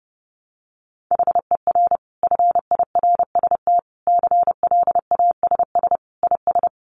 Morse code, heard as a keyed tone, about 28 wpm, 700 Hz. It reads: HER FIRST CLASS IS
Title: Additional Easy Phrases 30wpm